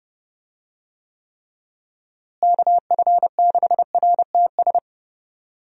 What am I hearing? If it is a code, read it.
Morse code, heard as a keyed tone, about 29 wpm, 700 Hz. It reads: KF6RTS
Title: USA Call Signs - 6 Characters - Repeated 2x - 30wpm